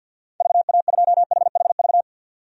Morse code, heard as a keyed tone, about 48 wpm, 700 Hz. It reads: VA2HHV
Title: DXCC-100 Call Signs - 6 Characters - 50wpm